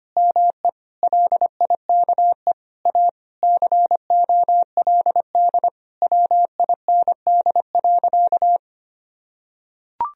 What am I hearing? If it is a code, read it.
Morse code, heard as a keyed tone, about 25 wpm, 700 Hz. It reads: ME LIKE A COLD WIND.
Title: Sentences from Top 1000 Words - Review Part 2 - 25wpm